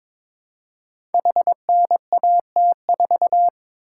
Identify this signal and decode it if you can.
Morse code, heard as a keyed tone, about 22 wpm, 700 Hz. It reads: HNAT4
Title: Code Groups - 5 Letter:Numbers - 22wpm